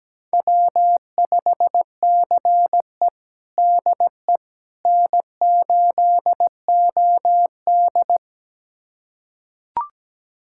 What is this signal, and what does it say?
Morse code, heard as a keyed tone, about 17 wpm, 700 Hz. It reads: W5CE DE N8OD
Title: USA Callsign DE Callsign - 4 Characters - 17wpm